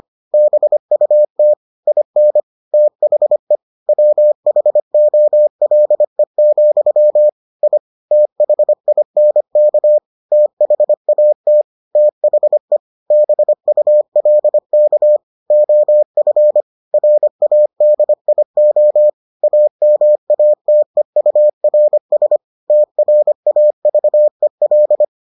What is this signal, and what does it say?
Morse code, heard as a keyed tone, about 25 wpm, 600 Hz. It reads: BUT IN THE WHOLE, I THINK THAT THE BULK OF RADIO AMATEURS TRAVEL